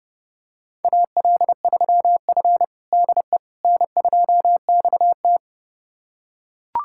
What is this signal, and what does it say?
Morse code, heard as a keyed tone, about 30 wpm, 700 Hz. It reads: AL3F DE N2XT
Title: USA Callsign DE Callsign - 4 Characters - 30wpm